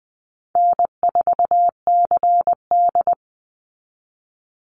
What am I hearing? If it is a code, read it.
Morse code, heard as a keyed tone, about 20 wpm, 700 Hz. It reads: N4CD
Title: Easy Call Signs - 20wpm